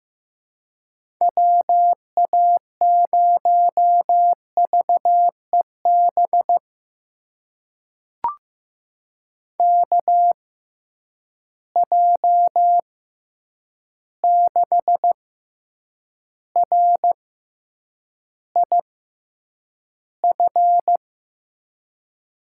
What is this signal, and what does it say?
Morse code, heard as a keyed tone, about 15 wpm, spaced as for 8 wpm, 700 Hz. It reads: WA0VEB KJ6RIF
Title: USA Call Signs - 6 Characters - 6x Character Spacing - 15wpm